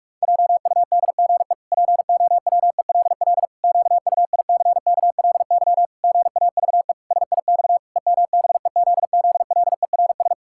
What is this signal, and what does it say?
Morse code, heard as a keyed tone, about 41 wpm, 700 Hz. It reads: JUDGE POWELL QUICKLY GAVE SIX EMBEZZLERS
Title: Pangram Challenge v2 - 45wpm